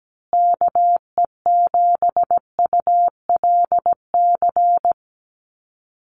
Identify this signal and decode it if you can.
Morse code, heard as a keyed tone, about 17 wpm, 700 Hz. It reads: KE7ULC
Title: USA Call Signs Repeated 2x 17wpm